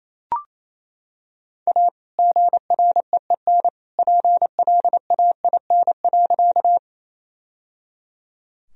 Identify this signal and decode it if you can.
Morse code, heard as a keyed tone, about 28 wpm, 700 Hz. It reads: A GREEN PLAIN.
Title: Sentences from Top 1000 Words - Review Part 2 - 28wpm